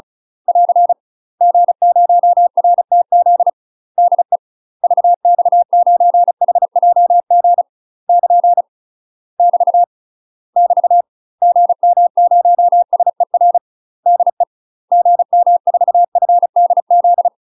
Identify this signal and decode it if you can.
Morse code, heard as a keyed tone, about 34 wpm, 700 Hz. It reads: <AR> G0RTZ DE VX9HJG <KN> <BT> <BT> GM0SER DE GM4FDZ
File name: UK MASTER QSOs TRAINING / 35WPM / UK MASTER QSOs 0040